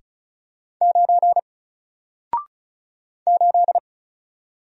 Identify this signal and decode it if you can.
Morse code, heard as a keyed tone, about 36 wpm, 700 Hz. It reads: 9 8
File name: Single-Letter-Number-35wpm